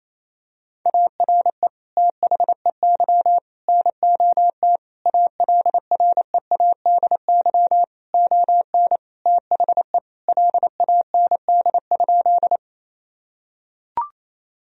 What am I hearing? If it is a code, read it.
Morse code, heard as a keyed tone, about 28 wpm, 700 Hz. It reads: ARE THEY NOT ALREADY ON THE LAND?
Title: Sentences from Top 1000 Words - Review Part 1 - 28wpm